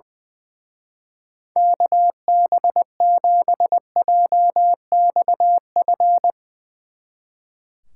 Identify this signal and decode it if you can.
Morse code, heard as a keyed tone, about 20 wpm, 700 Hz. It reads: KB7JXF
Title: Speed Racing - USA Call Signs 20wpm